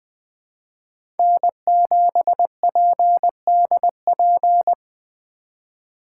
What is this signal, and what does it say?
Morse code, heard as a keyed tone, about 20 wpm, 700 Hz. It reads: N7PDP
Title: USA Call Signs - 2x Character Spacing - 20wpm